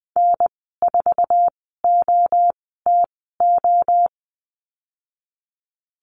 Morse code, heard as a keyed tone, about 20 wpm, spaced as for 14 wpm, 700 Hz. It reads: N4OTO
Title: 088 Lesson 15g - Number 5 - Call Signs 20wpm